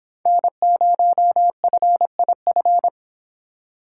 Morse code, heard as a keyed tone, about 26 wpm, 700 Hz. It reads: N0FIF